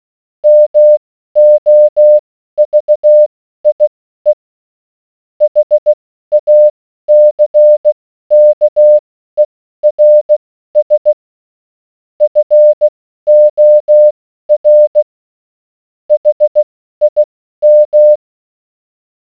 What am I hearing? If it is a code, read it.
Morse code, heard as a keyed tone, about 16 wpm, spaced as for 12 wpm, 600 Hz. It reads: MOVIE HACKERS FOR HIM